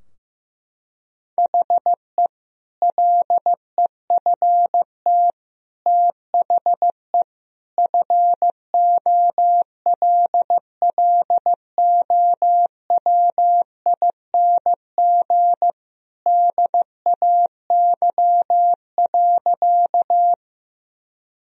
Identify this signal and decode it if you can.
Morse code, heard as a keyed tone, about 15 wpm, 700 Hz. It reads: HE LEFT THE FOLLOWING DAY.
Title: Sentences from Top 1000 Words - Review Part 4 - 15wpm